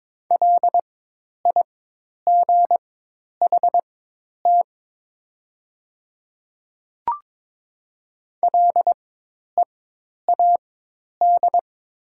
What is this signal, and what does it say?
Morse code, heard as a keyed tone, about 22 wpm, spaced as for 10 wpm, 700 Hz. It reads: LIGHT LEAD
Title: 4x Character Spacing - Top 500 Words - 22wpm